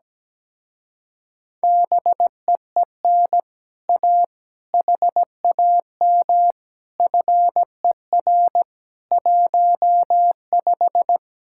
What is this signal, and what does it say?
Morse code, heard as a keyed tone, about 17 wpm, 700 Hz. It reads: BEEN A HAM FER 15
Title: Speed-Racing Common QSO Phrases - 17wpm